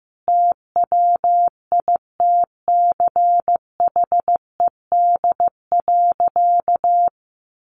Morse code, heard as a keyed tone, about 15 wpm, 700 Hz. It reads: TWITCHED.
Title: A Sound Of Thunder 15wpm